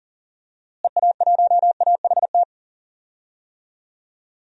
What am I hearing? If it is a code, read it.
Morse code, heard as a keyed tone, about 40 wpm, 700 Hz. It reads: EA1AHT
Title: DXCC-100 Call Signs - 6 Characters - Repeated 2x - 40wpm